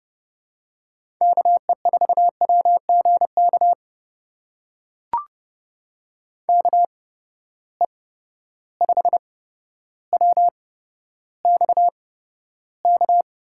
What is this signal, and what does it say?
Morse code, heard as a keyed tone, about 30 wpm, spaced as for 13 wpm, 700 Hz. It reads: KE4WGK KE5WXK
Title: USA Call Signs - 6 Characters - 8x Character Spacing - 30wpm